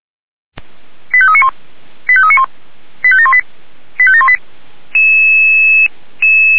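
Five-tone selective call (ZVEI), heard as quick runs of five tones, tones of about 70 ms. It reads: 85391 85391 85718 85718
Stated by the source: ZVEI five-tone call 89376, received by SDR.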